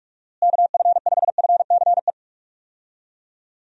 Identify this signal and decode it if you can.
Morse code, heard as a keyed tone, about 46 wpm, 700 Hz. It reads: KF5FCE
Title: USA Call Signs - 6 Characters - Repeated 2x - 45wpm